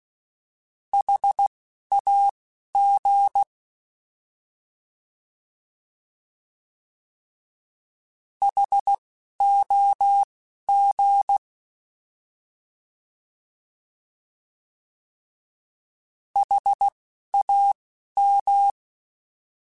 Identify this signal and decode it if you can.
Morse code, heard as a keyed tone, about 16 wpm, spaced as for 11 wpm, 800 Hz. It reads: HAG HOG HAM